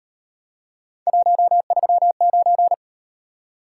Morse code, heard as a keyed tone, about 38 wpm, 700 Hz. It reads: 139